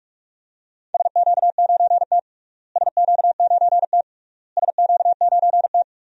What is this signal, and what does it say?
Morse code, heard as a keyed tone, about 48 wpm, 700 Hz. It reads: SQ9T SQ9T SQ9T
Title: DXCC-100 Call Signs - 4 Characters - Repeated 3x - 45wpm